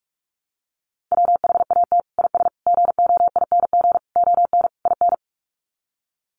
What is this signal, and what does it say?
Morse code, heard as a keyed tone, about 44 wpm, 700 Hz. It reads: WHAT IS GOING ON IN
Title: Sets of 5 Words Encore2 45wpm